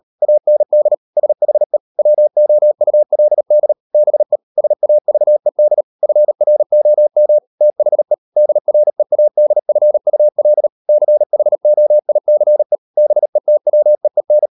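Morse code, heard as a keyed tone, about 38 wpm, 600 Hz. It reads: AND SHE WOULD BE SAVED FROM THE DREADFUL CHOICE BETWEEN
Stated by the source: Morse code practice at 38 WPM, sending Ken Follett's novel Never in English, from CW med Gnister.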